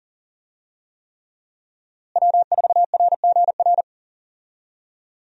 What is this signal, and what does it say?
Morse code, heard as a keyed tone, about 40 wpm, 700 Hz. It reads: W4RGR
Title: USA Call Signs - 5 Characters - Repeated 3x - 40wpm